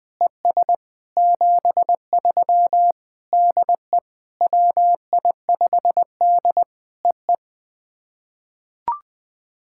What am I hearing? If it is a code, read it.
Morse code, heard as a keyed tone, about 20 wpm, 700 Hz. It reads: ES 73 DE WI5D EE
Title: POTA QSO Practice - 20wpm